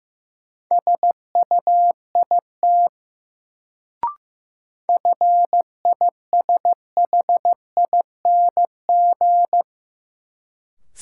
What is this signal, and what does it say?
Morse code, heard as a keyed tone, about 15 wpm, 700 Hz. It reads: SUIT FISHING
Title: Sentences from Top 1600 Words - New Word Drill Alt - 15wpm